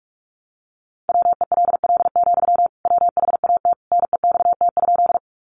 Morse code, heard as a keyed tone, about 44 wpm, 700 Hz. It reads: WELL, WHAT NEXT?
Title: Sentences from Top 300 Words 45wpm